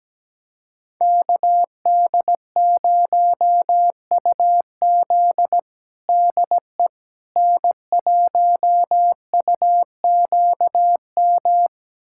Morse code, heard as a keyed tone, about 17 wpm, 700 Hz. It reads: KD0UZ DE N1UQM
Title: USA Callsign DE Callsign - 5 Characters - 17wpm